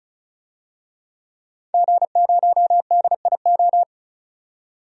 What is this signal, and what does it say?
Morse code, heard as a keyed tone, about 34 wpm, 700 Hz. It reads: G0DIO
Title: DXCC-100 Call Signs - 5 Characters - 35wpm